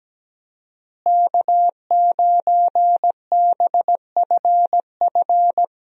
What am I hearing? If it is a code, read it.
Morse code, heard as a keyed tone, about 17 wpm, 700 Hz. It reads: K9BFF
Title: USA Call Signs - 5 Characters - Repeated 2x - 17wpm